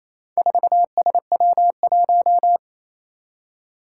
Morse code, heard as a keyed tone, about 28 wpm, 700 Hz. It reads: 4SW1
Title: Four Challenging Letter-Number Combinations 28wpm